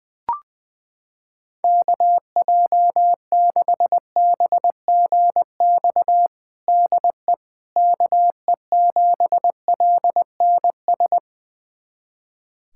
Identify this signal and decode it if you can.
Morse code, heard as a keyed tone, about 20 wpm, 700 Hz. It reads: KJ6BGX DE KE7LNS
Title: USA Callsign DE Callsign - 6 Characters - 20wpm